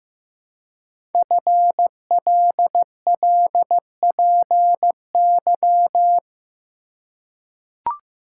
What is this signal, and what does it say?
Morse code, heard as a keyed tone, about 15 wpm, 700 Hz. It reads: FLLPY
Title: Code Groups - 5 Letters - 15wpm